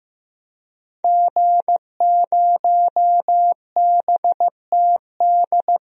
Morse code, heard as a keyed tone, about 15 wpm, 700 Hz. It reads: G0BTD
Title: DXCC-100 Call Signs - 5 Characters - 15wpm